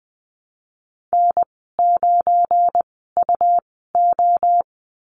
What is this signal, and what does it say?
Morse code, heard as a keyed tone, about 20 wpm, spaced as for 15 wpm, 700 Hz. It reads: N9UO
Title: 173 Lesson 28g - Number 9 - Call Signs 20wpm